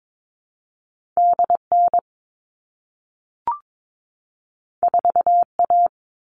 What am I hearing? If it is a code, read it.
Morse code, heard as a keyed tone, about 22 wpm, 700 Hz. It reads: DN 4A
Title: All Cut Numbers - 2-Digit 22wpm